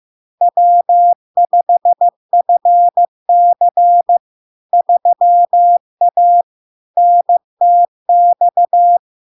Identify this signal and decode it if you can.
Morse code, heard as a keyed tone, about 15 wpm, 700 Hz. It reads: W5FC 3A NTX
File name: LARK-FD-2022-Log-in-CW-15wpm-700-Hz